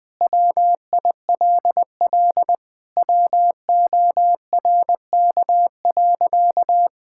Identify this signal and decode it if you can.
Morse code, heard as a keyed tone, about 20 wpm, 700 Hz. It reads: WILL WORK.
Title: Sentences with 5-Word Phrases - 20wpm